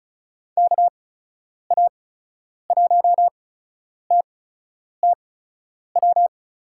Morse code, heard as a keyed tone, about 36 wpm, spaced as for 10 wpm, 700 Hz. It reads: KA1TTW
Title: USA Call Signs - 6 Characters - 8x Character Spacing - 35wpm